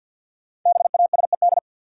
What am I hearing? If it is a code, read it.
Morse code, heard as a keyed tone, about 53 wpm, 700 Hz. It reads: BASED
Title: Sets of 2 Words - New Word Drill - 50wpm